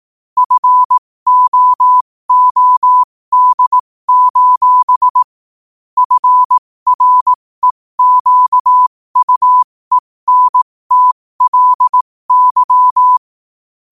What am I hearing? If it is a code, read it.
Morse code, heard as a keyed tone, about 18 wpm, spaced as for 15 wpm, 1000 Hz. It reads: FOOD: FREQUENTLY
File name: qotdfiles / 15wpm